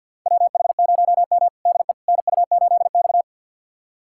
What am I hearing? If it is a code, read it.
Morse code, heard as a keyed tone, about 52 wpm, 700 Hz. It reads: WH0M DE NU8X
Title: USA Callsign DE Callsign - 4 Characters - 50wpm